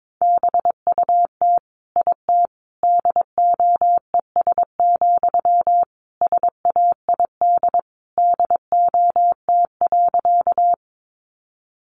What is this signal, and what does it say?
Morse code, heard as a keyed tone, about 22 wpm, 700 Hz. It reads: BUT IT DOES, SAID DOT.
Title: Sentences from Top 1500 Words 22wpm